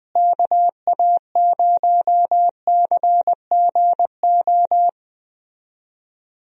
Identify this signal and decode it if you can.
Morse code, heard as a keyed tone, about 20 wpm, 700 Hz. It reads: KA0CGO
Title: USA Call Signs - 6 Characters - Repeated 3x - 20wpm